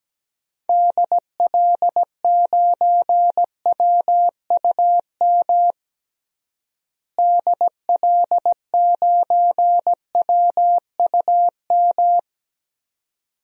Morse code, heard as a keyed tone, about 17 wpm, 700 Hz. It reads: DL9WUM DL9WUM
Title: DXCC-100 Call Signs - 6 Characters - Repeated 2x - 17wpm